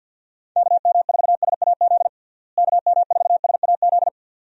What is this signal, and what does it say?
Morse code, heard as a keyed tone, about 51 wpm, 700 Hz. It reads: KM4SAZ KM4SAZ
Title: USA Call Signs - 6 Characters - Repeated 2x - 50wpm